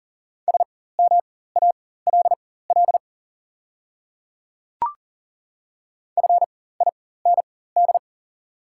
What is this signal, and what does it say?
Morse code, heard as a keyed tone, about 38 wpm, spaced as for 18 wpm, 690 Hz. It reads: SMALL FIND
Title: 4x Character Spacing - Top 300 Words - 40wpm